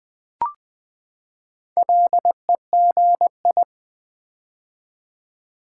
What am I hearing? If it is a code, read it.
Morse code, heard as a keyed tone, about 20 wpm, 700 Hz. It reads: LEGI
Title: Build A Word - 4 Letters At A Time - 11-Letter Words - 20wpm